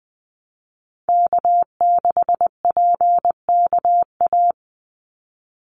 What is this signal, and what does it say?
Morse code, heard as a keyed tone, about 20 wpm, 700 Hz. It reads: K6PKA
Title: USA Call Signs 20wpm